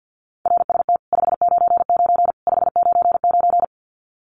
Morse code, heard as a keyed tone, about 49 wpm, 700 Hz. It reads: RST 599 599
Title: Common QSO Phrases 50wpm